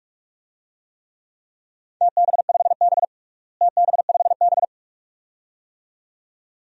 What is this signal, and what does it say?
Morse code, heard as a keyed tone, about 41 wpm, 700 Hz. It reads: TB5B TB5B
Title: DXCC-100 Call Signs - 4 Characters - Repeated 2x - 45wpm